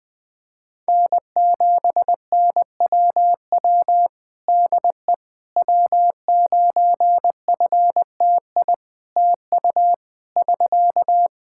Morse code, heard as a keyed tone, about 21 wpm, 700 Hz. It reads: N7NWW DE W9FTI TU <SK>
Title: Practice QSO - Exchange 4 - Ending - 20wpm